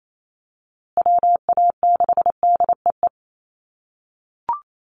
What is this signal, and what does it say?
Morse code, heard as a keyed tone, about 28 wpm, 700 Hz. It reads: WA6DEE
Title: USA Call Signs Repeated 2x 28wpm